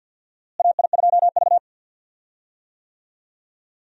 Morse code, heard as a keyed tone, about 46 wpm, 700 Hz. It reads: AI2V
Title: USA Call Signs - 4 Characters - Repeated 3x - 50wpm